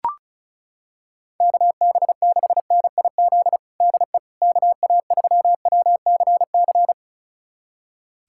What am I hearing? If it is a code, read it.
Morse code, heard as a keyed tone, about 36 wpm, 700 Hz. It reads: KB6NIZ DE KA3WCC
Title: USA Callsign DE Callsign - 6 Characters - 35wpm